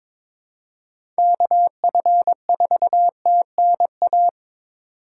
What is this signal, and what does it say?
Morse code, heard as a keyed tone, about 26 wpm, 700 Hz. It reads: KF4TNA
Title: Speed Racing - USA 2x3 Call Signs 22wpm